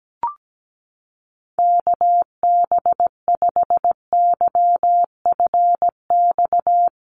Morse code, heard as a keyed tone, about 17 wpm, 700 Hz. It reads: KB5YFX
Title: USA Call Signs Repeated 3x 17wpm